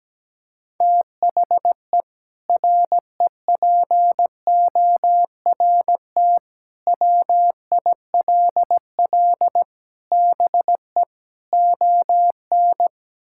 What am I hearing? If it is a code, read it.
Morse code, heard as a keyed tone, about 17 wpm, 700 Hz. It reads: THE REPORT WILL BE ON
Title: Sentences from Top 1000 Words - Review Part 6 - 17wpm